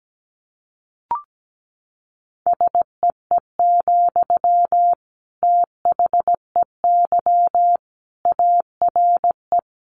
Morse code, heard as a keyed tone, about 17 wpm, 700 Hz. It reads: SEE, THEY ARE
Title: Sentences from Top 200 Words 17wpm